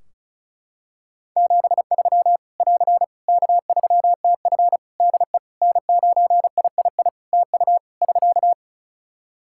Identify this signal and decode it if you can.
Morse code, heard as a keyed tone, about 36 wpm, 700 Hz. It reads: 73 <AR> K3TF DE N9III TU <SK>
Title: Practice QSO - Exchange 4 - Ending - 35wpm